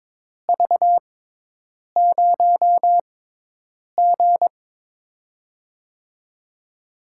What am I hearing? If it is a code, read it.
Morse code, heard as a keyed tone, about 22 wpm, spaced as for 8 wpm, 700 Hz. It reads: V0G
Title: USA Call Signs - 4 Characters - 6x Character Spacing - 22wpm